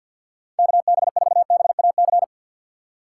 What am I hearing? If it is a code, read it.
Morse code, heard as a keyed tone, about 51 wpm, 700 Hz. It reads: KB4BAC
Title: USA Call Signs - 6 Characters - 3x Character Spacing - 50wpm